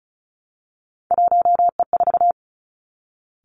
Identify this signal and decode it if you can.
Morse code, heard as a keyed tone, about 36 wpm, 700 Hz. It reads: 1E4
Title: Three Challenging Letter-Number Combinations 35wpm